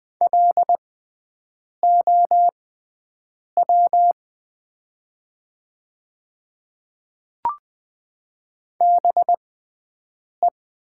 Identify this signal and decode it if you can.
Morse code, heard as a keyed tone, about 20 wpm, spaced as for 7 wpm, 700 Hz. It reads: LOW BE
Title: 6x Character Spacing - Top 300 Words - 20wpm